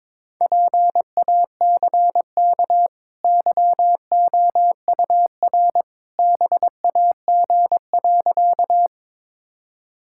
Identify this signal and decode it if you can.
Morse code, heard as a keyed tone, about 21 wpm, 700 Hz. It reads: PACK YOUR BAG.
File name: practice / 22wpm / Easy Sentences with Hints - 22wpm